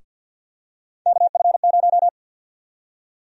Morse code, heard as a keyed tone, about 52 wpm, 700 Hz. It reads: KF0
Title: USA Call Sign Prefixes - Repeated 2x - 50wpm